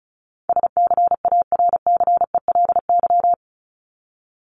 Morse code, heard as a keyed tone, about 34 wpm, 700 Hz. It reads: SCARCELY
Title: LY Suffix - Top 100 Words with Base Words 35wpm